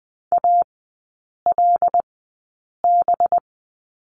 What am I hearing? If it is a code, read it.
Morse code, heard as a keyed tone, about 20 wpm, spaced as for 8 wpm, 700 Hz. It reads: ALB
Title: 184 Lesson 30e - Letter b - Three Characters 20wpm